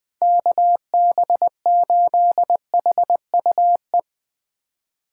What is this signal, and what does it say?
Morse code, heard as a keyed tone, about 21 wpm, 700 Hz. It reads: KB8HUE
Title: Speed Racing - USA 2x3 Call Signs 17wpm